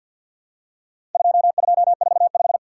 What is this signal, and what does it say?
Morse code, heard as a keyed tone, about 50 wpm, 700 Hz. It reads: 2245
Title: Worked All Europe DX Contest - QTC Exchange with Repeats - 50wpm